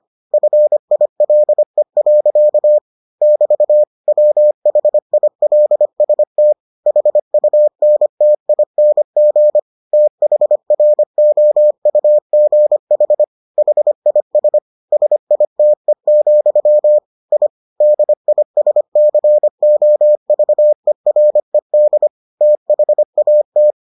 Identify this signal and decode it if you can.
Morse code, heard as a keyed tone, about 25 wpm, 600 Hz. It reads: FILE. <BT> WHILST HUNTING THROUGH HIS SITE, I DISCOVERED THAT